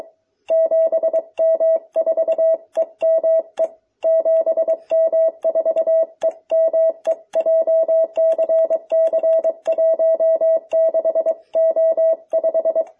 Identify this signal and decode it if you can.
Morse code, heard as a keyed tone, about 22 wpm, 660 Hz. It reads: 7M4EME 7M4EME JCC16O5